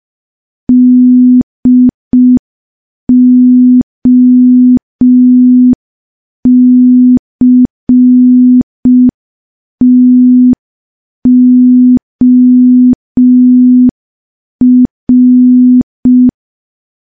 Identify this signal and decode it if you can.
Morse code, heard as a keyed tone, about 5 wpm, 250 Hz. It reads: DOCTOR
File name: morse / morse-training-05WPM